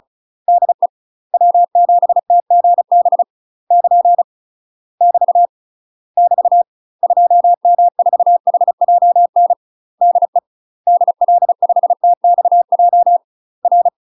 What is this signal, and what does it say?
Morse code, heard as a keyed tone, about 34 wpm, 700 Hz. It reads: DE W7TGB <KN> <BT> <BT> 2M4HJN DE DL5TXJ R
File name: UK MASTER QSOs TRAINING / 35WPM / UK MASTER QSOs 0034